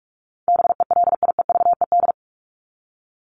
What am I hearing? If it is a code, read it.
Morse code, heard as a keyed tone, about 44 wpm, 700 Hz. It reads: BELIEVED
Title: ED Suffix - Top 100 Words with Base Words 45wpm